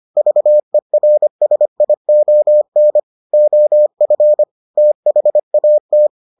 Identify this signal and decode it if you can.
Morse code, heard as a keyed tone, about 25 wpm, 600 Hz. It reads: VERSION OF THAT